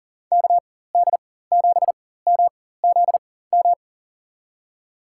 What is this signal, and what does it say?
Morse code, heard as a keyed tone, about 40 wpm, spaced as for 19 wpm, 700 Hz. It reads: KD7MZM
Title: USA Call Signs - 6 Characters - 4x Character Spacing - 40wpm